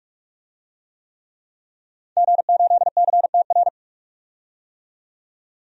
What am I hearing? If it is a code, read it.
Morse code, heard as a keyed tone, about 46 wpm, 700 Hz. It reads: G8CTR
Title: DXCC-100 Call Signs - 5 Characters - 45wpm